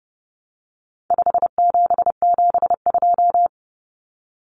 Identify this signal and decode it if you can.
Morse code, heard as a keyed tone, about 30 wpm, 700 Hz. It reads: 5772
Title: Common Cut Numbers - 4-Digit 30wpm